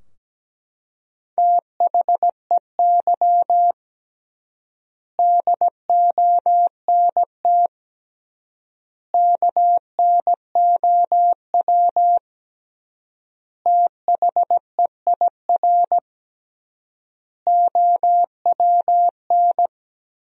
Morse code, heard as a keyed tone, about 17 wpm, 700 Hz. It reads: THEY DONT KNOW THEIR OWN
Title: 3x Word-Spacing - Sentences from Top 300 Words - 17wpm